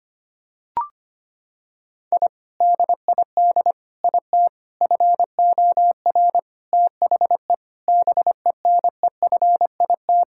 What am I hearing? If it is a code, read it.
Morse code, heard as a keyed tone, about 25 wpm, 700 Hz. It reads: I DID IT FOR THE BENEFIT
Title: Sentences from Top 1000 Words - Review Part 6 - 25wpm